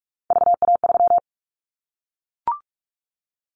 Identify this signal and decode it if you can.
Morse code, heard as a keyed tone, about 44 wpm, 700 Hz. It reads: VA3
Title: Callsign Prefixes for Top 25 DXCC - Repeated 3x - 45wpm